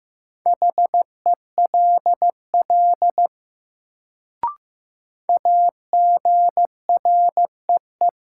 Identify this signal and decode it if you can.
Morse code, heard as a keyed tone, about 15 wpm, 700 Hz. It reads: HELL AGREE
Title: Sets of 5 Words Encore - New Word Drill - 15wpm